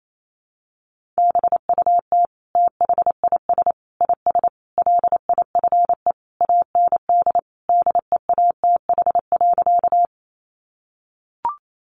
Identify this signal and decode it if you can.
Morse code, heard as a keyed tone, about 28 wpm, 700 Hz. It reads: BUT THIS IS LIFE AND DEATH.
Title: Sentences from Top 1100 Words 28wpm